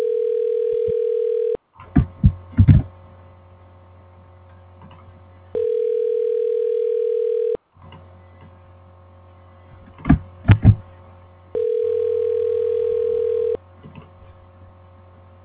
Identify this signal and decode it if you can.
Ringback tone (North America), 440 + 480 Hz.